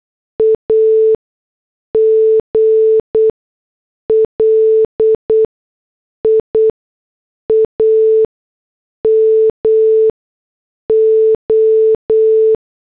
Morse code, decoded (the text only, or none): AGLIAMO